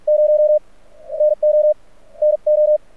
Morse code, heard as a keyed tone, about 48 wpm, spaced as for 41 wpm, 600 Hz. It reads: 0 NO NO
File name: archived / CRSnet